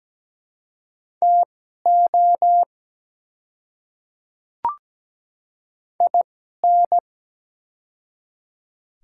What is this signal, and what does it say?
Morse code, heard as a keyed tone, about 17 wpm, spaced as for 12 wpm, 700 Hz. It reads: TO IN
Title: 2x Character Spacing - Top 300 Words - 17wpm